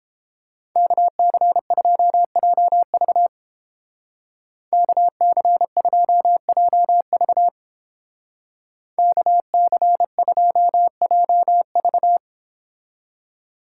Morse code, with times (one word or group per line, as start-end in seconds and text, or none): 0.8-3.3 KC2JV
4.7-7.5 KC2JV
9.0-12.2 KC2JV